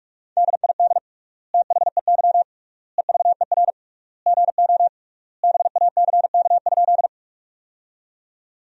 Morse code, heard as a keyed tone, about 41 wpm, 700 Hz. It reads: DID THEY EVER GO BACK?
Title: 3x Word-Spacing - Sentences from Top 400 Words - 45wpm